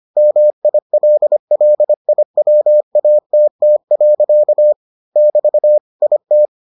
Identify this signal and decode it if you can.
Morse code, heard as a keyed tone, about 25 wpm, 600 Hz. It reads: MILLIWATT. <BT> IT